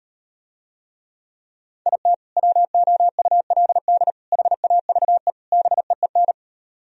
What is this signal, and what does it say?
Morse code, heard as a keyed tone, about 38 wpm, 700 Hz. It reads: IT WOULD HAVE BEEN